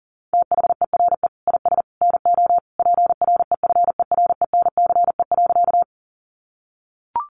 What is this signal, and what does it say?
Morse code, heard as a keyed tone, about 40 wpm, 700 Hz. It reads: THERE IS NO PREFERENCE.
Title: Sentences from Top 1500 Words 40wpm